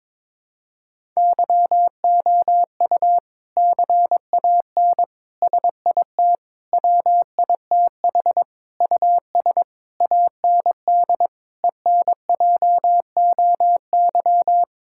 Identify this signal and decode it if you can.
Morse code, heard as a keyed tone, about 22 wpm, 700 Hz. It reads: YOU CAN SIT WITH US AND ENJOY
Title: Sentences from Top 1000 Words - Review Part 5 - 22wpm